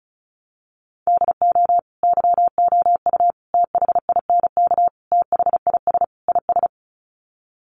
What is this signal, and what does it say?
Morse code, heard as a keyed tone, about 34 wpm, 700 Hz. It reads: DO YOU THINK THIS IS
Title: Sets of 5 Words Encore2 35wpm